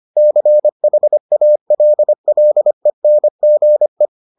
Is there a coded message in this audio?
Yes: Morse code from 0.2 to 4.1 s, heard as a keyed tone, about 25 wpm, 600 Hz. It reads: CHALLENGE